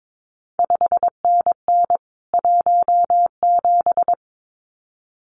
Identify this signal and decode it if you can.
Morse code, heard as a keyed tone, about 22 wpm, 700 Hz. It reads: 5NN 17
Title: CQ WW DX Contest Exchange Encore - 22wpm